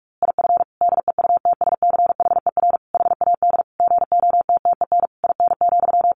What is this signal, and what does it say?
Morse code, heard as a keyed tone, about 44 wpm, 700 Hz. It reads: IF DEUTSCHER HAD GOTTEN IN,
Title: A Sound Of Thunder 45wpm